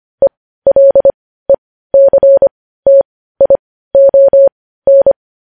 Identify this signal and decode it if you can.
Morse code, heard as a keyed tone, about 25 wpm, spaced as for 15 wpm, 550 Hz. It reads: ELECTION
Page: news headlines in morse code at 15 wpm